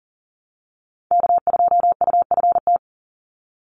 Morse code, heard as a keyed tone, about 40 wpm, 700 Hz. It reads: K2UFT
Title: Call Signs - 40wpm